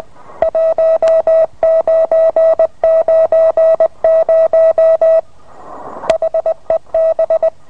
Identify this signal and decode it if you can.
Morse code, heard as a keyed tone, about 20 wpm, 650 Hz. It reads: 1990 HEB